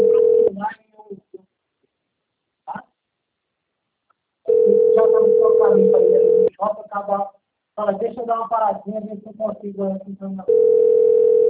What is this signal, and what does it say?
Ringback tone (North America), 440 + 480 Hz, with speech.
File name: out- - - - - .21609